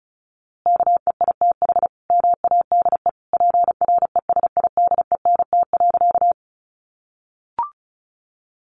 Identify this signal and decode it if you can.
Morse code, heard as a keyed tone, about 34 wpm, 700 Hz. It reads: KEITH MADE PRESIDENT.
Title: A Sound Of Thunder 35wpm